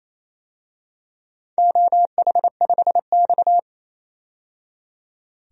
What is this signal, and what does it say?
Morse code, heard as a keyed tone, about 29 wpm, 700 Hz. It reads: OH5X
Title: DXCC-100 Call Signs - 4 Characters - 28wpm